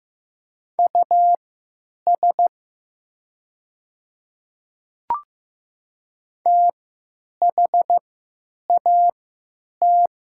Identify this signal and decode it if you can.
Morse code, heard as a keyed tone, about 16 wpm, spaced as for 9 wpm, 700 Hz. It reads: US THAT
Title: 3x Character Spacing - Top 200 Words - 15wpm